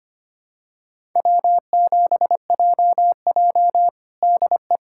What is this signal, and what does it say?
Morse code, heard as a keyed tone, about 25 wpm, 700 Hz. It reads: W7JJ DE
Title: USA Callsign DE Callsign - 4 Characters - 25wpm